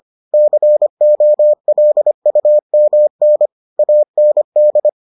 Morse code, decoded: COLUMN AND

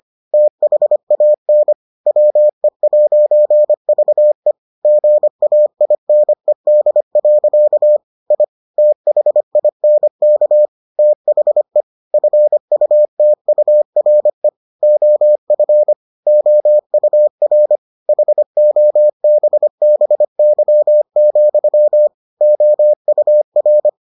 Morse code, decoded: THAN WE'VE GAINED. I THINK THE FUTURE OF OUR HOBBY, OUR

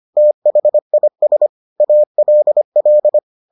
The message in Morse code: THIS ALL

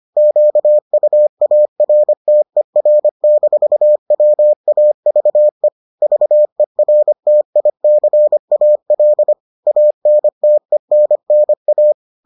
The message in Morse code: QUARTER-WAVE VERTICAL ANTENNA

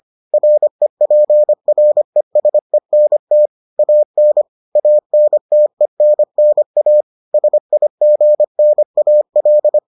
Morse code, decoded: REPRESENT AN ANTENNA SIGNAL

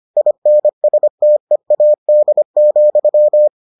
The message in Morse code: INSTEAD,

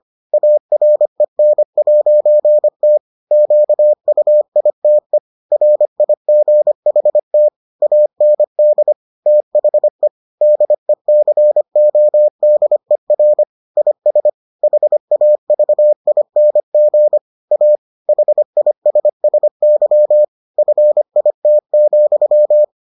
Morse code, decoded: AREN'T QUITE RIGHT AND THE DECODER IS HAVING A HISSY FIT,

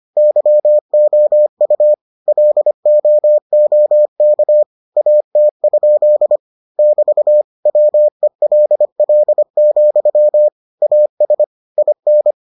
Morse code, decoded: YOU LOOK AT? <BT> WELL, AS IN

